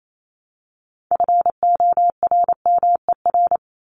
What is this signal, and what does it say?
Morse code, heard as a keyed tone, about 28 wpm, 700 Hz. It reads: FORMER
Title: LY Suffix - Top 100 Words with Base Words 28wpm